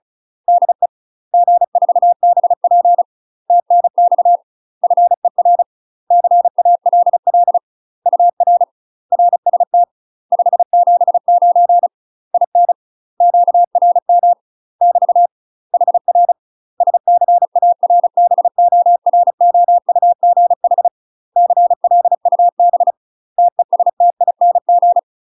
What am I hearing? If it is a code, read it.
Morse code, heard as a keyed tone, about 34 wpm, 700 Hz. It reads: DE G4BP TNX FER CALL UR RST 579 IN QRM <BT> HR SCARBOROUGH CLUB TESTING
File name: UK MASTER QSOs TRAINING / 35WPM / UK MASTER QSOs 0021